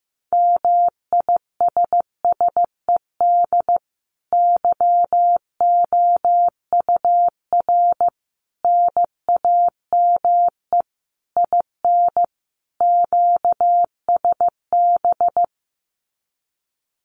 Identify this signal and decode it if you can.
Morse code, heard as a keyed tone, about 15 wpm, 700 Hz. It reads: MISSED YOUR NAME IN QSB
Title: Common QSO Phrases 15wpm